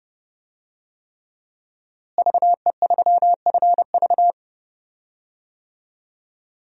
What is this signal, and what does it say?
Morse code, heard as a keyed tone, about 30 wpm, 700 Hz. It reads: VE3FV
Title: DXCC-100 Call Signs - 5 Characters - Repeated 3x - 30wpm